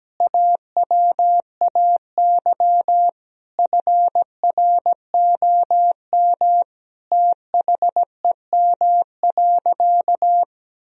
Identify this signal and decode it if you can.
Morse code, heard as a keyed tone, about 17 wpm, 700 Hz. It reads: AWAY FROM THEM.
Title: Sentences from Top 1000 Words - Review Part 2 - 17wpm